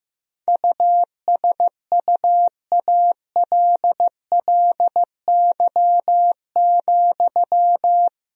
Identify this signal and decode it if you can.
Morse code, heard as a keyed tone, about 16 wpm, 700 Hz. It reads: USUALLY,
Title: Sentences from Top 1000 Words - Review Part 3 - 15wpm